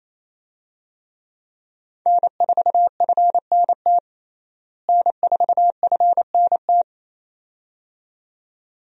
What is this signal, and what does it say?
Morse code, heard as a keyed tone, about 28 wpm, 700 Hz. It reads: N4FNT N4FNT
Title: USA Call Signs - 5 Characters - Repeated 2x - 28wpm